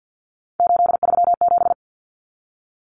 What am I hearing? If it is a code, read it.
Morse code, heard as a keyed tone, about 48 wpm, 700 Hz. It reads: 837
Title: Numbers - 3-Digit 50wpm